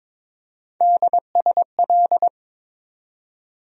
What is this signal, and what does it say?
Morse code, heard as a keyed tone, about 22 wpm, 700 Hz. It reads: DSL
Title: USA Call Sign Postfixes - 3 Letters 22wpm